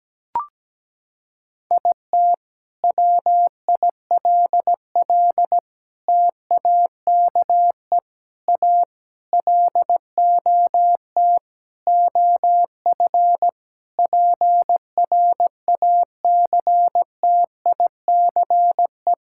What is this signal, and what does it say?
Morse code, heard as a keyed tone, about 17 wpm, 700 Hz. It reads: IT WILL TAKE A LOT OF PRACTICE
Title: Sentences from Top 1000 Words - Review Part 6 - 17wpm